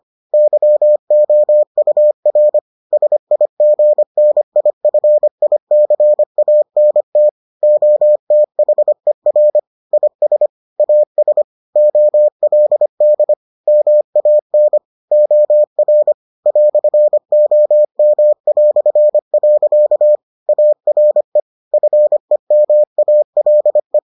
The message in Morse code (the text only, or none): YOUR SIGNIFICANT OTHER IS AS OLD MAN OR "OM". ARE FEMALE